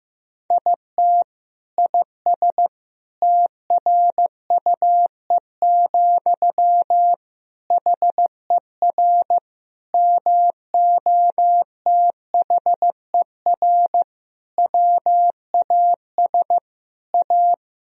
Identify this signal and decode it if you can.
Morse code, heard as a keyed tone, about 15 wpm, 700 Hz. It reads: IT IS TRUE, HER MOTHER WAS A
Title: Sentences from Top 1000 Words - Review Part 2 - 15wpm